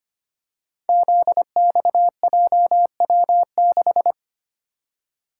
Morse code, heard as a keyed tone, about 25 wpm, 700 Hz. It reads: ZXJW6
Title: Code Groups - 5 Letter:Numbers - 25wpm